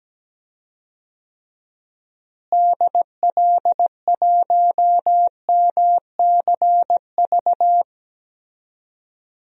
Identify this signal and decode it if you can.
Morse code, heard as a keyed tone, about 17 wpm, 700 Hz. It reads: DL1MCV
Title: DXCC-100 Call Signs - 6 Characters - Repeated 3x - 17wpm